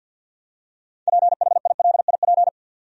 Morse code, heard as a keyed tone, about 50 wpm, 700 Hz. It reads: PHILIP
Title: USA Common Names - 50wpm